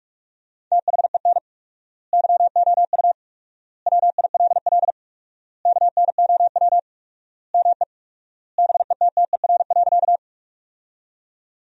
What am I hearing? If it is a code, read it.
Morse code, heard as a keyed tone, about 41 wpm, 700 Hz. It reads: THEN YOU WILL KNOW ME BETTER.
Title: 4x Word-Spacing - Sentences from Top 400 Words - 45wpm